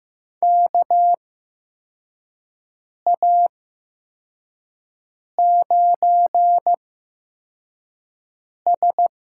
Morse code, heard as a keyed tone, about 15 wpm, spaced as for 4 wpm, 700 Hz. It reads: KA9S
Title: USA Call Signs - 8x Character Spacing - 15wpm